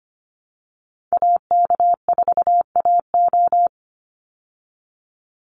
Morse code, heard as a keyed tone, about 25 wpm, 700 Hz. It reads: AK4AO
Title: Call Signs Repeated 3x - 25wpm